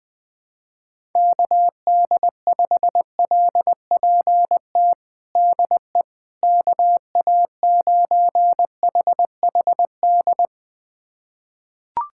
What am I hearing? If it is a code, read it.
Morse code, heard as a keyed tone, about 20 wpm, 700 Hz. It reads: KD5LPT DE KA9HHD
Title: USA Callsign DE Callsign 20wpm